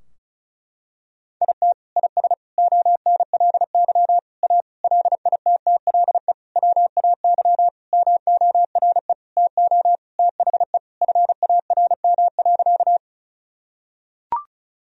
Morse code, heard as a keyed tone, about 34 wpm, 700 Hz. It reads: IT IS ONLY A LITTLE WAY MORE TO THE FARM.
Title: Sentences from Top 1000 Words - Review Part 2 - 35wpm